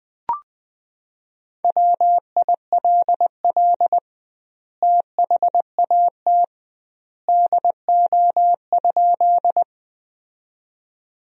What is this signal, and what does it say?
Morse code, heard as a keyed tone, about 20 wpm, 700 Hz. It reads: WILL THAT DO?
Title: 2x Word-Spacing - Sentences from Top 100 Words - 20wpm